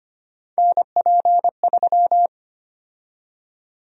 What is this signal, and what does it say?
Morse code, heard as a keyed tone, about 25 wpm, 700 Hz. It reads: NP3
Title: USA Call Sign Prefixes - 25wpm